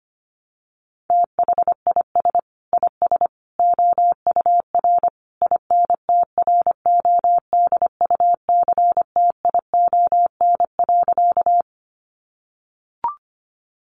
Morse code, heard as a keyed tone, about 25 wpm, 700 Hz. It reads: THIS IS OUR INTRODUCTION.
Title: Sentences from Top 1000 Words 25wpm